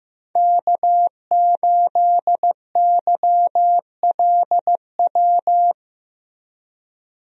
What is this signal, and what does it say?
Morse code, heard as a keyed tone, about 15 wpm, 700 Hz. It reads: K8YLW
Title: USA Call Signs - 5 Characters - Repeated 3x - 15wpm